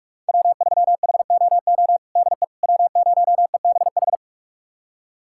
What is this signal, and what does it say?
Morse code, heard as a keyed tone, about 42 wpm, 700 Hz. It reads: W3HOO DE W0EBH
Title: USA Callsign DE Callsign - 5 Characters - 45wpm